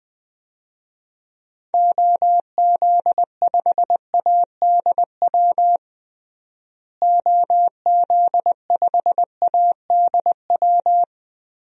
Morse code, heard as a keyed tone, about 21 wpm, 700 Hz. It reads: OZ5ADW OZ5ADW
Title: DXCC-100 Call Signs - 6 Characters - Repeated 3x - 20wpm